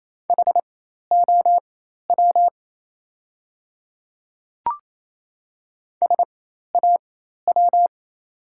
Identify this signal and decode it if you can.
Morse code, heard as a keyed tone, about 28 wpm, spaced as for 13 wpm, 700 Hz. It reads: HOW SAW
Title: 4x Character Spacing - Top 500 Words - 28wpm